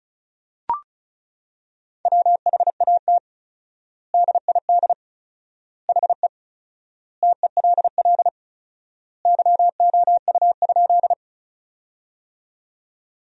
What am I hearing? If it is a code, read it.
Morse code, heard as a keyed tone, about 36 wpm, 700 Hz. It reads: WHAT DID HE TELL YOU?
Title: 4x Word-Spacing - Sentences from Top 200 Words - 35wpm